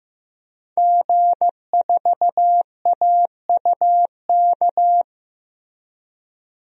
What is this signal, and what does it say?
Morse code, heard as a keyed tone, about 15 wpm, 700 Hz. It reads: G4AUK
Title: DXCC-100 Call Signs - 5 Characters - Repeated 3x - 15wpm